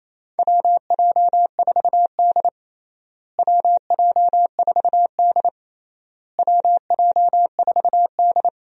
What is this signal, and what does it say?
Morse code, heard as a keyed tone, about 28 wpm, 700 Hz. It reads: WJ4D WJ4D WJ4D
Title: USA Call Signs - 4 Characters - Repeated 3x - 28wpm